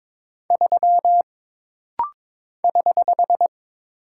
Morse code, heard as a keyed tone, about 22 wpm, 700 Hz. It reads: 3 <HH>
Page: This rapid-fire timing is geared to push your ICR (Instant Character Recognition) skills.